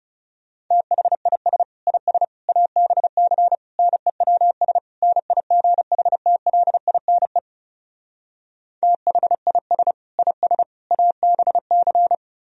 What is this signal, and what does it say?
Morse code, heard as a keyed tone, about 37 wpm, 700 Hz. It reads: THIS IS ABC NEWS NIGHTLINE THIS IS ABC
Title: Speed-Racing - Sets of 5 Words 30wpm